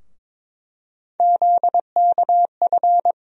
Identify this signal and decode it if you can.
Morse code, heard as a keyed tone, about 21 wpm, 700 Hz. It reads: ZKF
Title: USA Call Sign Postfixes - 3 Letters 22wpm